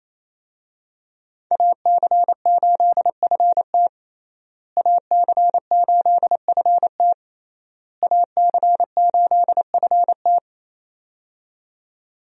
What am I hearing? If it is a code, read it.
Morse code, heard as a keyed tone, about 28 wpm, 700 Hz. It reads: AC8FT AC8FT AC8FT
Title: USA Call Signs - 5 Characters - Repeated 3x - 28wpm